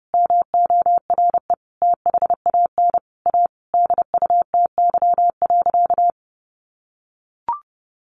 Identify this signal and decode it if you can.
Morse code, heard as a keyed tone, about 30 wpm, 700 Hz. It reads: MORE THAN A DUTY.
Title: Sentences from Top 1500 Words 30wpm